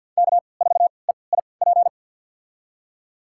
Morse code, heard as a keyed tone, about 46 wpm, spaced as for 28 wpm, 700 Hz. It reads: K4EIP
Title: USA Call Signs - 5 Characters - 3x Character Spacing - 50wpm